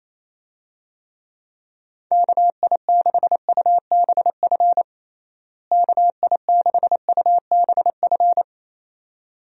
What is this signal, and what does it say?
Morse code, heard as a keyed tone, about 28 wpm, 700 Hz. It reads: KI6UBF KI6UBF
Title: USA Call Signs - 6 Characters - Repeated 2x - 28wpm